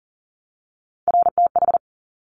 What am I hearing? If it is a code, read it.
Morse code, heard as a keyed tone, about 41 wpm, 700 Hz. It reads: RTH
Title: Top Three Letter Combinations in Words 40wpm